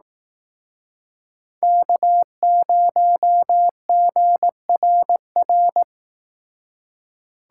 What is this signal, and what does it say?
Morse code, heard as a keyed tone, about 18 wpm, 700 Hz. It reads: K0GRR
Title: Speed Racing - USA 1x3 Call Signs 15wpm